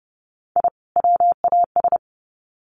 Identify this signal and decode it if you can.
Morse code, heard as a keyed tone, about 30 wpm, 700 Hz. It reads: I WAS
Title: Sets of 2 Words 30wpm